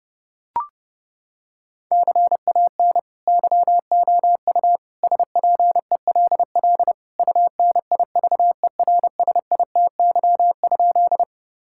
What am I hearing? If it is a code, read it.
Morse code, heard as a keyed tone, about 30 wpm, 700 Hz. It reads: CAN YOU SPELL UNIVERSITY?
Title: Sentences from Top 1000 Words - Review Part 5 - 30wpm